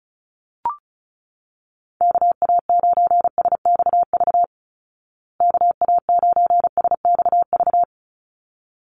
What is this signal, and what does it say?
Morse code, heard as a keyed tone, about 34 wpm, 700 Hz. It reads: KA9SXV KA9SXV
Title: Call Signs Repeated 3x - 35wpm